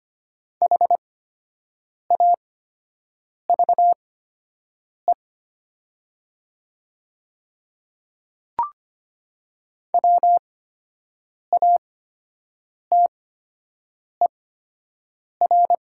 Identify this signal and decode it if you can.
Morse code, heard as a keyed tone, about 25 wpm, spaced as for 7 wpm, 700 Hz. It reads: HAVE WATER
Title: 8x Character Spacing - Top 200 Words - 25wpm